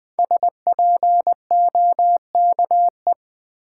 Morse code, heard as a keyed tone, about 20 wpm, 700 Hz. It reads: SPOKE
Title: Sentences from Top 1200 Words - New Word Drill - 20wpm